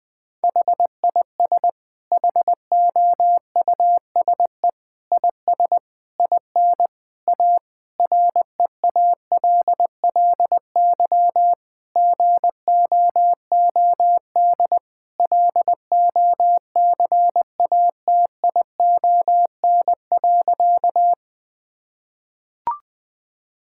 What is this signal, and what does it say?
Morse code, heard as a keyed tone, about 20 wpm, 700 Hz. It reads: HIS HOUSE IS IN A REALLY GOOD LOCATION.
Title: Sentences from Top 1000 Words - Review Part 6 - 20wpm